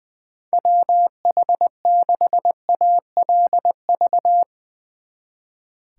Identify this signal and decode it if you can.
Morse code, heard as a keyed tone, about 22 wpm, 700 Hz. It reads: WH6ALV
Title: Speed Racing - USA 2x3 Call Signs 20wpm